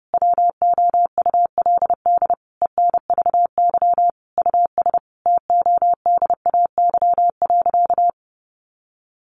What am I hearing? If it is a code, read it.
Morse code, heard as a keyed tone, about 28 wpm, 700 Hz. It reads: WOULD ENVY US TODAY.
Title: A Sound Of Thunder 30wpm